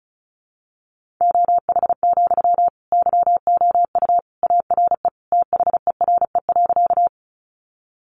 Morse code, heard as a keyed tone, about 34 wpm, 700 Hz. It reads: OH, YOU ARE THERE.
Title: Sentences from Top 500 Words 35wpm